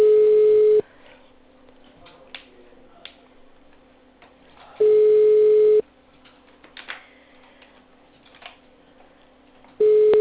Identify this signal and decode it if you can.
Ringback tone (Europe, 425 Hz).